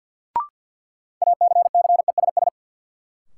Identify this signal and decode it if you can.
Morse code, heard as a keyed tone, about 53 wpm, 700 Hz. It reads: ACCESS
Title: Sentences from Top 1000 Words - Review Part 4 - New Word Drill - 50wpm